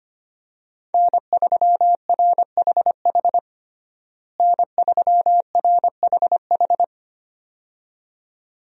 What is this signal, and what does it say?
Morse code, heard as a keyed tone, about 25 wpm, 700 Hz. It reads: N3RHH N3RHH
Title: USA Call Signs - 5 Characters - Repeated 2x - 25wpm